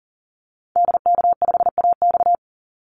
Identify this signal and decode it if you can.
Morse code, heard as a keyed tone, about 40 wpm, 700 Hz. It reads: DK5AX
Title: Call Signs Repeated 3x - 40wpm